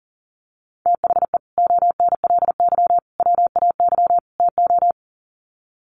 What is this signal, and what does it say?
Morse code, heard as a keyed tone, about 40 wpm, 700 Hz. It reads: THE ONLY WAY TO
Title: Sets of 4 Words 40wpm